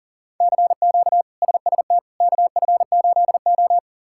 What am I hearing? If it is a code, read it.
Morse code, heard as a keyed tone, about 40 wpm, 700 Hz. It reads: CQ SST KF8O
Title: K1USN SST QSO Practice - 40wpm